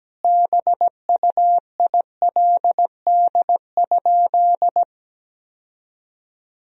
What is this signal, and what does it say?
Morse code, heard as a keyed tone, about 17 wpm, 700 Hz. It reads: BUILD?
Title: Sentences from Top 1000 Words - Review Part 3 - 17wpm